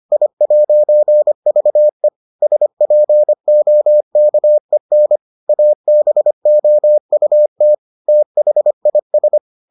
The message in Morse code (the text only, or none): I'VE SPOKEN ABOUT THIS